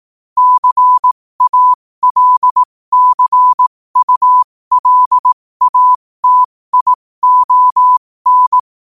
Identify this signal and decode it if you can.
Morse code, heard as a keyed tone, about 18 wpm, spaced as for 15 wpm, 1000 Hz. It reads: CALCULATION
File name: qotdfiles / 15wpm